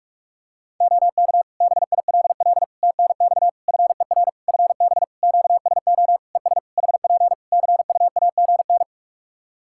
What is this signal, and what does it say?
Morse code, heard as a keyed tone, about 44 wpm, 700 Hz. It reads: OK BILL TNX FER FB QSO ES HP CUAGN
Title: Practice QSO - Exchange 4 - Ending - 45wpm